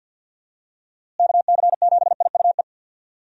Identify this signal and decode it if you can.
Morse code, heard as a keyed tone, about 50 wpm, 700 Hz. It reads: KC7IUE
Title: USA Call Signs - 6 Characters - Repeated 3x - 50wpm